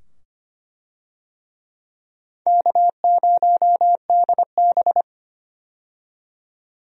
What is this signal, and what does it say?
Morse code, heard as a keyed tone, about 25 wpm, 700 Hz. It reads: K0DB
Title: USA Call Signs - 4 Characters - 25wpm